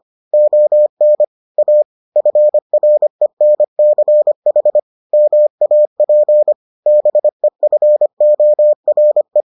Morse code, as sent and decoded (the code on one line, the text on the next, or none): --- -. / .- / ..-. .-. -. -.-. .... / -- .- .--. / -... ..-. --- .-. .
ON A FRENCH MAP BEFORE